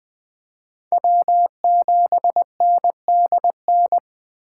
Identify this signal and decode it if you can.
Morse code, heard as a keyed tone, about 25 wpm, 700 Hz. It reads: W7NDN
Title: Speed Racing - USA Call Signs 20wpm